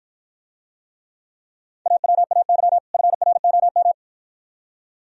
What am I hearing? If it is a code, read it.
Morse code, heard as a keyed tone, about 52 wpm, 700 Hz. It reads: AWAY FROM